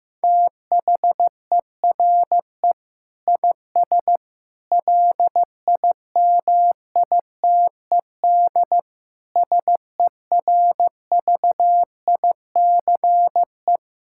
Morse code, heard as a keyed tone, about 15 wpm, 700 Hz. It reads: THERE IS LIMITED SERVICE
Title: Sentences from Top 1000 Words - Review Part 6 - 15wpm